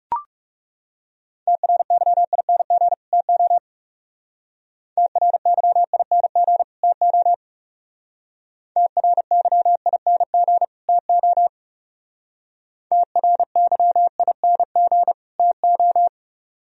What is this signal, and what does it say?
Morse code, heard as a keyed tone, about 37 wpm, 700 Hz. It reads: TRYING TO TRYING TO TRYING TO TRYING TO
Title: Speed-Racing - Sets of 2 Words 30wpm